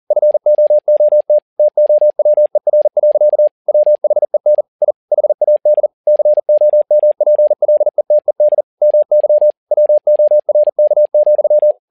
Morse code, heard as a keyed tone, about 40 wpm, 600 Hz. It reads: FOOT TOWER. WHEN I HAD COMPLETED MY WORK,
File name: learning center / tower trouble / tower touble 40wpm